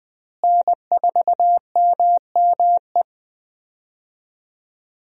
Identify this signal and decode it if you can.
Morse code, heard as a keyed tone, about 20 wpm, 700 Hz. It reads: N4MME
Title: USA Call Signs - 5 Characters - Repeated 3x - 20wpm